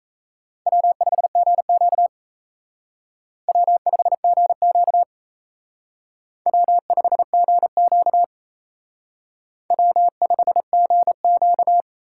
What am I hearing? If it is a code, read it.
Morse code, heard as a keyed tone, about 35 wpm, 700 Hz. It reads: W5GQ W5GQ W5GQ W5GQ
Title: Speed Racing - USA 1x2 Call Signs 28wpm